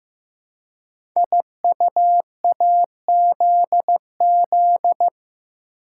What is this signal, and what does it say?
Morse code, heard as a keyed tone, about 15 wpm, 700 Hz. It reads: IUAZZ
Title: Code Groups - 5 Letters - 15wpm